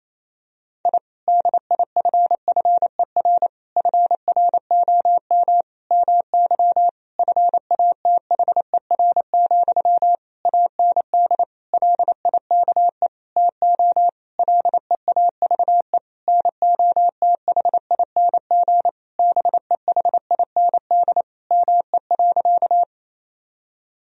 Morse code, heard as a keyed tone, about 28 wpm, 700 Hz. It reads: I DIFFER FROM MY FATHER, AND LIKE TO LEAVE NOTHING BEHIND ME.
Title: Sentences from Top 1000 Words - Review Part 1 - 28wpm